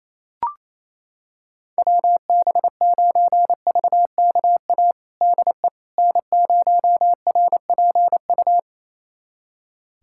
Morse code, heard as a keyed tone, about 28 wpm, 700 Hz. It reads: WB9VKA DE N0RPU
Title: USA Callsign DE Callsign 28wpm